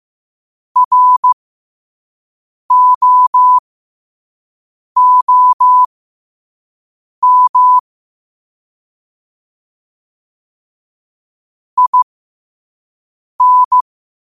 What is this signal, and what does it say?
Morse code, heard as a keyed tone, about 15 wpm, spaced as for 5 wpm, 1000 Hz. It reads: ROOM IN
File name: qotdfiles / 5wpm